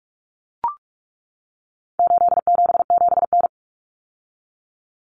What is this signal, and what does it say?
Morse code, heard as a keyed tone, about 44 wpm, 700 Hz. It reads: 877N
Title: Common Cut Numbers - 4-Digit 45wpm